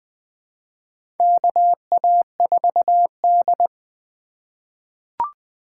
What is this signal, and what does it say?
Morse code, heard as a keyed tone, about 20 wpm, 700 Hz. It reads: KA4D
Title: USA Call Signs - 4 Characters - 4x Character Spacing - 20wpm